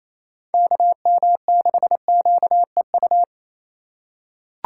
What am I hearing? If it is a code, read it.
Morse code, heard as a keyed tone, about 27 wpm, 700 Hz. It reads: KM6QEU
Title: USA Call Signs - 3x Character Spacing - 28wpm